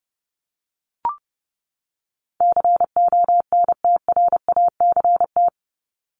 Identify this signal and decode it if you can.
Morse code, heard as a keyed tone, about 30 wpm, 700 Hz. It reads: CONTRACT
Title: Top 1251-1500 Words 30wpm